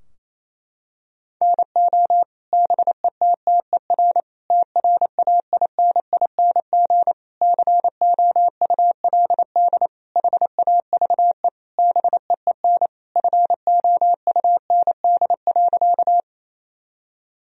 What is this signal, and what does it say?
Morse code, heard as a keyed tone, about 28 wpm, 700 Hz. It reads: NO BETTER TRAINING COULD HAVE BEEN FOUND.
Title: Sentences from Top 1000 Words - Review Part 4 - 28wpm